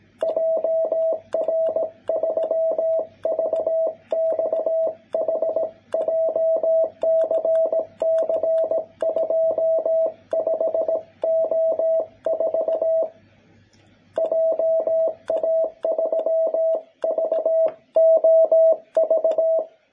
Morse code, heard as a keyed tone, about 18 wpm, 660 Hz. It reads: JR3VXH JCC25O4 JA3VOV
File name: jh8cbh / 2021cw